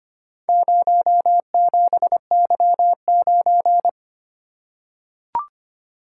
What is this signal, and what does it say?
Morse code, heard as a keyed tone, about 25 wpm, 700 Hz. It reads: 07Y9
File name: practice / 25wpm / Four Letter-Number Combinations - 25wpm